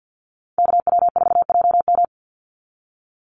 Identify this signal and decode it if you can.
Morse code, heard as a keyed tone, about 50 wpm, 700 Hz. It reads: KW4JM
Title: NA SOTA Chaser Callsigns 50wpm